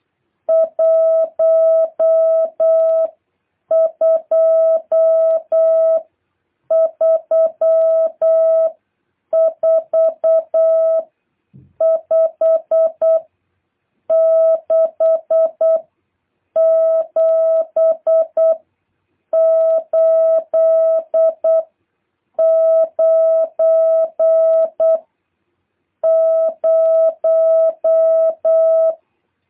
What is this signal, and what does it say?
Morse code, heard as a keyed tone, about 8 wpm, spaced as for 6 wpm, 650 Hz. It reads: 1234567890
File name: jh8cbh / 2023cw